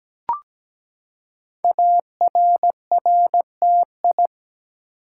Morse code, heard as a keyed tone, about 17 wpm, 700 Hz. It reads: ARRTI
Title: Code Groups - 5 Letters - 17wpm